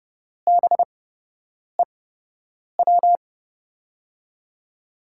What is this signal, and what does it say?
Morse code, heard as a keyed tone, about 32 wpm, spaced as for 8 wpm, 700 Hz. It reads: BEW
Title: USA Call Signs - 5 Characters - 8x Character Spacing - 30wpm